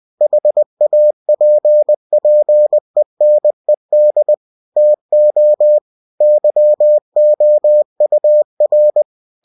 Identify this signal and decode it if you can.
Morse code, heard as a keyed tone, about 20 wpm, 600 Hz. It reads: HAPPENED TO YOUR
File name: cw-books / winnie-the-pooh / winnie-20wpm-page-0026